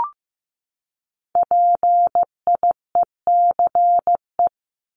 Morse code, heard as a keyed tone, about 15 wpm, 700 Hz. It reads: PIECE
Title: Top 500 Words 15wpm